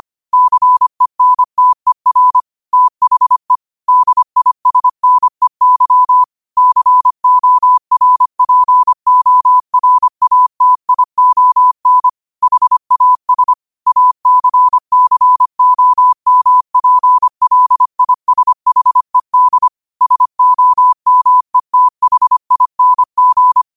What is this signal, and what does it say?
Morse code, heard as a keyed tone, about 25 wpm, 1000 Hz. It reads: CENTER THE DISNEY CORPORATION HAS ACCOMPLISHED SOMETHING
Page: Quotes for Fri, 15 Aug 2025 in Morse Code at 25 words per minute.